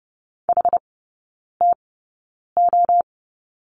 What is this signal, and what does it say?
Morse code, heard as a keyed tone, about 30 wpm, spaced as for 9 wpm, 700 Hz. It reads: HTO
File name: Lesson 17e - Letter u - Three Characters 30wpm